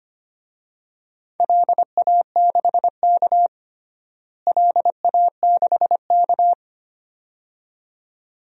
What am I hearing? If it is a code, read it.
Morse code, heard as a keyed tone, about 25 wpm, 700 Hz. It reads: LA6K LA6K
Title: DXCC-100 Call Signs - 4 Characters - Repeated 2x - 25wpm